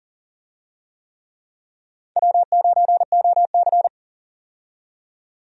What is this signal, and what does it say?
Morse code, heard as a keyed tone, about 43 wpm, 690 Hz. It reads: W9OC
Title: USA Call Signs - 4 Characters - 40wpm